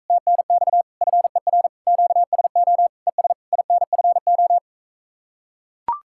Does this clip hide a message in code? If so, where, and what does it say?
Yes: Morse code from 0.1 to 4.6 s, heard as a keyed tone, about 43 wpm, 700 Hz. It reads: TNX FER QSO ES INFO